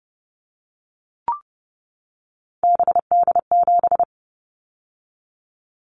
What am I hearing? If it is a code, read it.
Morse code, heard as a keyed tone, about 30 wpm, 700 Hz. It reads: BD7
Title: Callsign Prefixes for Top 100 DXCC - 30wpm